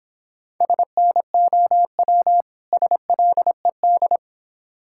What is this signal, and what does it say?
Morse code, heard as a keyed tone, about 25 wpm, 700 Hz. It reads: SNOW SLED
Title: Difficult Words - Speed-Racing with Hints - 20wpm